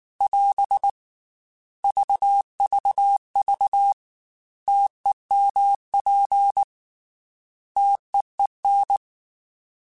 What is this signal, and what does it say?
Morse code, heard as a keyed tone, about 19 wpm, 800 Hz. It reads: <AS> VVV TEMP TEEN
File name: T Prefixes / T mp3s